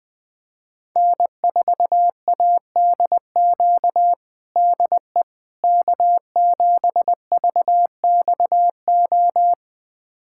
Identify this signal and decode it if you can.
Morse code, heard as a keyed tone, about 20 wpm, 700 Hz. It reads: N4ADQ DE K7VXO
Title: USA Callsign DE Callsign - 5 Characters - 20wpm